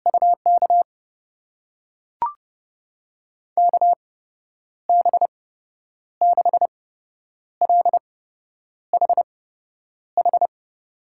Morse code, heard as a keyed tone, about 32 wpm, spaced as for 9 wpm, 700 Hz. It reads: UK KB6LHH
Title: USA Call Signs - 6 Characters - 8x Character Spacing - 30wpm